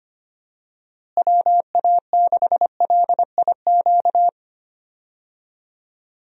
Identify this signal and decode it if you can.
Morse code, heard as a keyed tone, about 25 wpm, 700 Hz. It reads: WA6LIQ
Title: USA Call Signs - 6 Characters - Repeated 2x - 25wpm